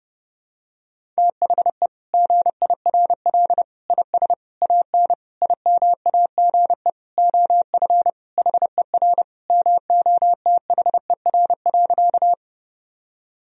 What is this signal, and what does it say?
Morse code, heard as a keyed tone, about 28 wpm, 700 Hz. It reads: THE GIRL IS AN IMAGE OF HER MOTHER.
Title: Sentences from Top 1000 Words - Review Part 4 - 30wpm